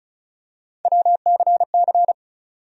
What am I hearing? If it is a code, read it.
Morse code, heard as a keyed tone, about 36 wpm, 700 Hz. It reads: WCC
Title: USA Call Sign Postfixes - 3 Letters 35wpm